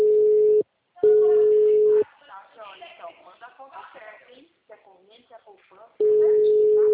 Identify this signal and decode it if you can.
Ringback tone (Europe, 425 Hz), with speech.